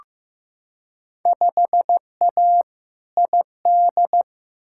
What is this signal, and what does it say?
Morse code, heard as a keyed tone, about 15 wpm, 700 Hz. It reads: 5A ID
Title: Field Day Exchanges Encore - 15wpm